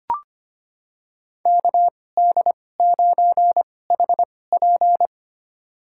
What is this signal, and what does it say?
Morse code, heard as a keyed tone, about 25 wpm, spaced as for 18 wpm, 700 Hz. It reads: KD9HP
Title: USA Call Signs - 2x Character Spacing - 25wpm